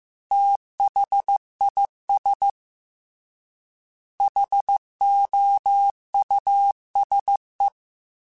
Morse code, heard as a keyed tone, about 15 wpm, 800 Hz. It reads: THIS HOUSE
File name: T then H Runs / T then H Runs.mp3s / T then H Runs @13